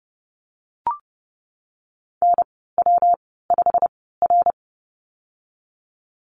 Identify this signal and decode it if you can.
Morse code, heard as a keyed tone, about 30 wpm, 700 Hz. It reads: N W 5 R
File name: Lesson 19g - Letter w - Call Signs 30wpm